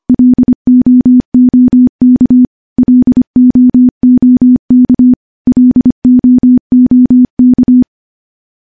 Morse code, heard as a keyed tone, about 25 wpm, 250 Hz. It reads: LOOK LOOK LOOK
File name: morse-training-25WPM